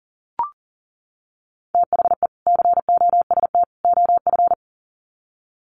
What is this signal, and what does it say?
Morse code, heard as a keyed tone, about 40 wpm, 700 Hz. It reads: THE COST OF
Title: Sets of 3 Words 40wpm